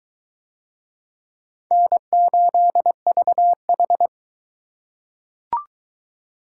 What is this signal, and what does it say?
Morse code, heard as a keyed tone, about 23 wpm, 700 Hz. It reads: N8VH